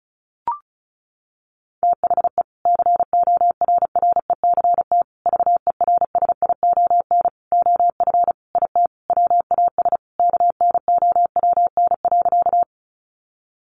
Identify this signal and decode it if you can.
Morse code, heard as a keyed tone, about 34 wpm, 700 Hz. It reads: THE CORRECT VERSION OF IT WAS KNOWN.
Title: Sentences from Top 1400 Words 35wpm